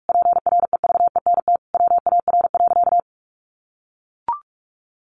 Morse code, heard as a keyed tone, about 44 wpm, 700 Hz. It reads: PREVENT WAR.
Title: Sentences from Top 1100 Words 45wpm